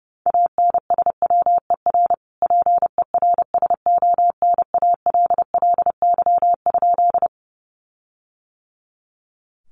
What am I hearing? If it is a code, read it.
Morse code, heard as a keyed tone, about 30 wpm, 700 Hz. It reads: ANSWER PERSONALLY?
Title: Sentences from Top 1400 Words 30wpm